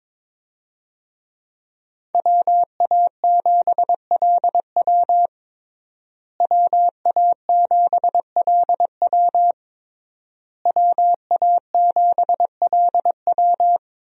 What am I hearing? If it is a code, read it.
Morse code, heard as a keyed tone, about 22 wpm, 700 Hz. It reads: WA7LW WA7LW WA7LW
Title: USA Call Signs - 5 Characters - Repeated 3x - 22wpm